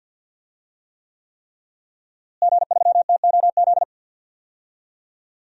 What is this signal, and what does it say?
Morse code, heard as a keyed tone, about 47 wpm, 700 Hz. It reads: G3TOZ
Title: DXCC-100 Call Signs - 5 Characters - Repeated 3x - 50wpm